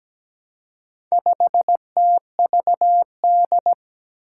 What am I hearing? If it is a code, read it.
Morse code, heard as a keyed tone, about 17 wpm, 700 Hz. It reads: 5TVD
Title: Four Challenging Letter-Number Combinations 17wpm